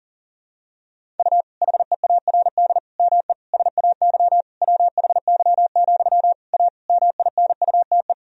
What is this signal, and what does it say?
Morse code, heard as a keyed tone, about 40 wpm, 700 Hz. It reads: U HEARD ME SAY WHY, A MINUTE
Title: Sentences from Top 500 Words and Abbreviations 40wpm